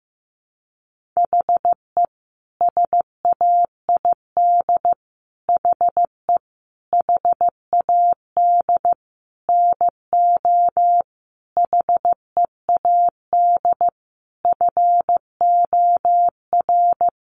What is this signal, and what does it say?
Morse code, heard as a keyed tone, about 15 wpm, 700 Hz. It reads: HE SAID HE HAD NO HEAD FOR
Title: Sentences from Top 1500 Words 15wpm